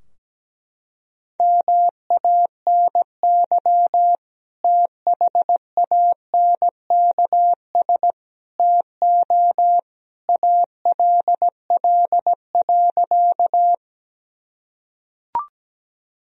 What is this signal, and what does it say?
Morse code, heard as a keyed tone, about 17 wpm, 700 Hz. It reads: MANY THANKS TO ALL.
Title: Sentences from Top 1000 Words - Review Part 3 - 17wpm